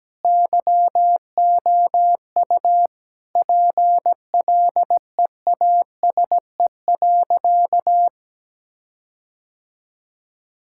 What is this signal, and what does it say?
Morse code, heard as a keyed tone, about 17 wpm, 700 Hz. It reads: YOU PLEASE.
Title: Sentences from Top 1600 Words 17wpm